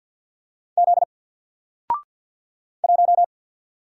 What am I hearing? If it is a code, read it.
Morse code, heard as a keyed tone, about 48 wpm, 700 Hz. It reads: Z 1
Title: Single Character (Letter:Number:Punctuation:Prosign) 50wpm